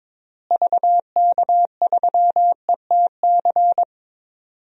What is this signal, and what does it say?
Morse code, heard as a keyed tone, about 22 wpm, 700 Hz. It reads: VK3ETC
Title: DXCC-100 Call Signs - 6 Characters - Repeated 3x - 22wpm